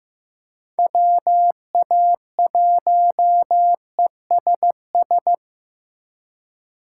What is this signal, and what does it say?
Morse code, heard as a keyed tone, about 15 wpm, 700 Hz. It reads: WA1ESS
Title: USA Call Signs - 2x Character Spacing - 15wpm